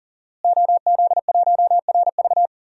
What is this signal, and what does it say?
Morse code, heard as a keyed tone, about 40 wpm, 700 Hz. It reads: OZ1RV
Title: DXCC-100 Call Signs - 5 Characters - Repeated 2x - 40wpm